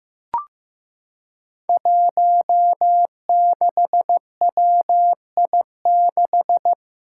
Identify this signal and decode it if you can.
Morse code, heard as a keyed tone, about 15 wpm, 700 Hz. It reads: 16WI6
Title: Five Challenging Letter-Number Combinations 15wpm